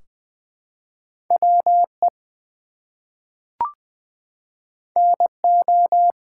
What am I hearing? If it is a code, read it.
Morse code, heard as a keyed tone, about 20 wpm, 700 Hz. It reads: WE NO
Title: Two Letter Words - 20wpm